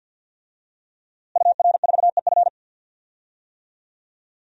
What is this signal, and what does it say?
Morse code, heard as a keyed tone, about 53 wpm, 700 Hz. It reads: UR4EF
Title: DXCC-100 Call Signs - 5 Characters - Repeated 2x - 50wpm